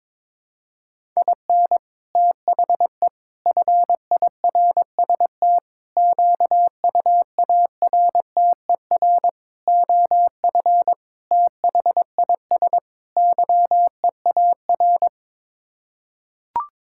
Morse code, heard as a keyed tone, about 22 wpm, 700 Hz. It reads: IN THE FIRST QUARTER OF THIS YEAR
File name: practice / 22wpm / Sets of 7 Words - 22wpm